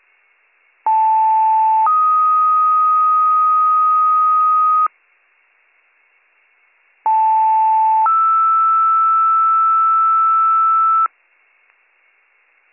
Two-tone page, heard as a one-second tone, then a three-second tone, about 880 then 1250 Hz, about 880 then 1320 Hz. It is heard from 0.9 to 11.1 s.